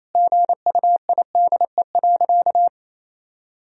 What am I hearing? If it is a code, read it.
Morse code, heard as a keyed tone, about 28 wpm, 700 Hz. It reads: GUIDE.
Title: Sentences from Top 1000 Words - Review Part 3 - 28wpm